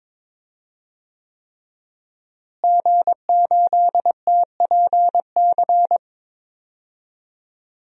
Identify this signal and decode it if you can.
Morse code, heard as a keyed tone, about 22 wpm, 700 Hz. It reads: G8TPC
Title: DXCC-100 Call Signs - 5 Characters - Repeated 3x - 22wpm